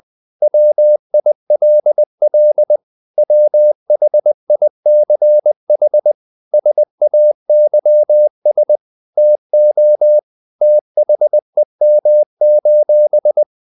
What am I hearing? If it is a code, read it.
Morse code, heard as a keyed tone, about 20 wpm, 600 Hz. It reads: WILL WHICH SAYS TO THEM:
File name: CW / Practice / If20